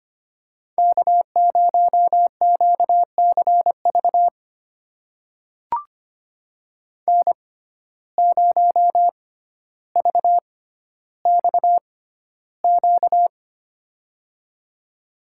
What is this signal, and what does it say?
Morse code, heard as a keyed tone, about 25 wpm, spaced as for 13 wpm, 700 Hz. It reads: K0QCV N0VXQ
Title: USA Call Signs - 5 Characters - 6x Character Spacing - 25wpm